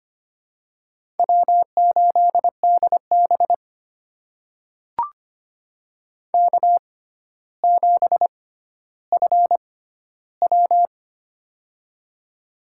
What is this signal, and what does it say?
Morse code, heard as a keyed tone, about 25 wpm, spaced as for 13 wpm, 700 Hz. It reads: W8DB K7FW
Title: USA Call Signs - 4 Characters - 6x Character Spacing - 25wpm